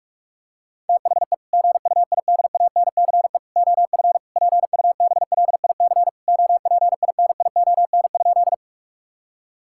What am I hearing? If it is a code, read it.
Morse code, heard as a keyed tone, about 46 wpm, 700 Hz. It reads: THE GUIDANCE OF PUBLIC OPINION?
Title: Sentences from Top 1600 Words 45wpm